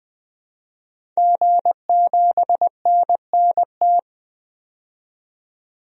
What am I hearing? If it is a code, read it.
Morse code, heard as a keyed tone, about 20 wpm, 700 Hz. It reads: G7NNT
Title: DXCC-100 Call Signs - 5 Characters - 20wpm